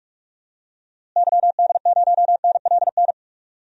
Morse code, heard as a keyed tone, about 46 wpm, 700 Hz. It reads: YD0NLN
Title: DXCC-100 Call Signs - 6 Characters - Repeated 3x - 45wpm